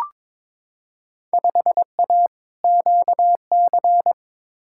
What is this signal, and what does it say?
Morse code, heard as a keyed tone, about 22 wpm, 700 Hz. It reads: 5A QC
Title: Field Day Exchanges Encore - 22wpm